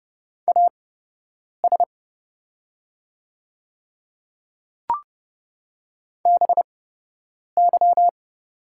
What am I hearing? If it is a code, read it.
Morse code, heard as a keyed tone, about 30 wpm, spaced as for 8 wpm, 700 Hz. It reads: AS BY
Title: 8x Character Spacing - Top 100 Words - 30wpm